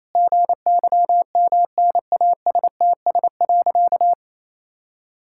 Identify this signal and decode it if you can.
Morse code, heard as a keyed tone, about 28 wpm, 700 Hz. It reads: GYMNASTS.
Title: Pangram Challenge v2 - 28wpm